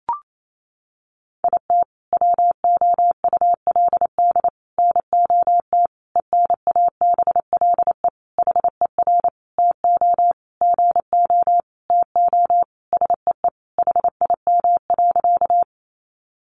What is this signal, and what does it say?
Morse code, heard as a keyed tone, about 28 wpm, 700 Hz. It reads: IT WOULD NOT ENABLE HER TO GO TO SEE HIM.
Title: Sentences from Top 1500 Words 28wpm